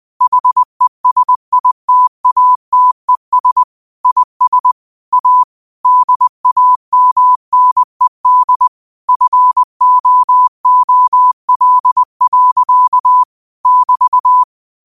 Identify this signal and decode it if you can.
Morse code, heard as a keyed tone, about 20 wpm, 1000 Hz. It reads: HESITATES IS A DAMNED FOOL. <BT>